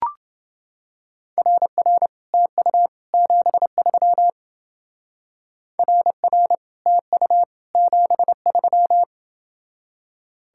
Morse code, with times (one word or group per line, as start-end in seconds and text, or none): 1.4-2.1 RR
2.3-2.9 TU
3.1-4.3 73
5.8-6.5 RR
6.9-7.4 TU
7.7-9.0 73